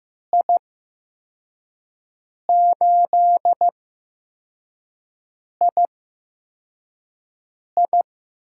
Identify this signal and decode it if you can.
Morse code, heard as a keyed tone, about 15 wpm, spaced as for 4 wpm, 700 Hz. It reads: I8II
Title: USA Call Signs - 5 Characters - 8x Character Spacing - 15wpm